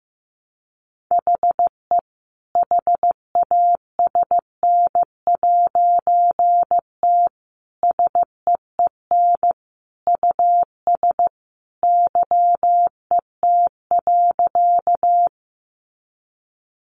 Morse code, heard as a keyed tone, about 15 wpm, 700 Hz. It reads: HE HASN'T SEEN US YET.
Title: A Sound Of Thunder 15wpm